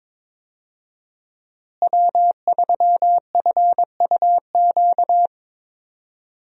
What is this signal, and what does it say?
Morse code, heard as a keyed tone, about 22 wpm, 700 Hz. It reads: W3FUQ
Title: USA Call Signs - 5 Characters - Repeated 2x - 22wpm